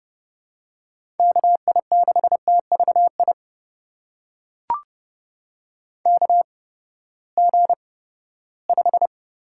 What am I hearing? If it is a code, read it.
Morse code, heard as a keyed tone, about 30 wpm, spaced as for 17 wpm, 700 Hz. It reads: KI6TVI KG5
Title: USA Call Signs - 6 Characters - 8x Character Spacing - 30wpm